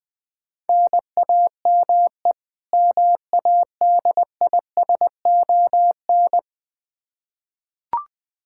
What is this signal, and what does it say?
Morse code, heard as a keyed tone, about 20 wpm, 700 Hz. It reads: NAME MADISON
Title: Practice QSO - Exchange 1 - Beginning - 20wpm